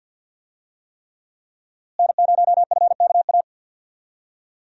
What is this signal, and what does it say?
Morse code, heard as a keyed tone, about 50 wpm, 700 Hz. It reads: N0FKA
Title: USA Call Signs - 5 Characters - Repeated 3x - 50wpm